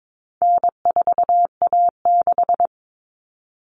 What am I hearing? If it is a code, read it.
Morse code, heard as a keyed tone, about 22 wpm, 700 Hz. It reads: N4A6
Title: All Cut Numbers - 4-Digit 22wpm